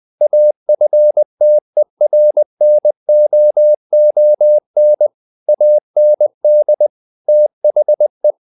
Morse code, heard as a keyed tone, about 20 wpm, 600 Hz. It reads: AFTERNOON AND THE
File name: the old man and the sea / the old man and the sea part 1 20wpm